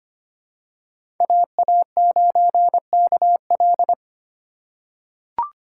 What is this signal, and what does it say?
Morse code, heard as a keyed tone, about 25 wpm, 700 Hz. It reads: AA9KL
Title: USA Call Signs - 5 Characters - 8x Character Spacing - 25wpm